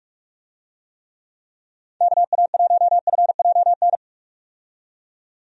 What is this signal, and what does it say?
Morse code, heard as a keyed tone, about 46 wpm, 700 Hz. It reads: KA1FJN
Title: USA Call Signs - 6 Characters - 45wpm